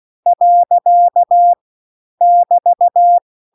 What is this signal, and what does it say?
Morse code, heard as a keyed tone, about 16 wpm, 700 Hz. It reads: . <BT>
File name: CW-The-Big-Twoheaded-River-2-16WPM